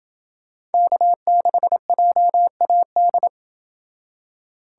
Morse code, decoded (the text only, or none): K6JAD